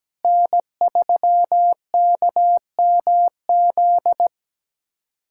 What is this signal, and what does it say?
Morse code, heard as a keyed tone, about 17 wpm, 700 Hz. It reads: N3KMZ
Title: USA Callsign DE Callsign - 5 Characters - 17wpm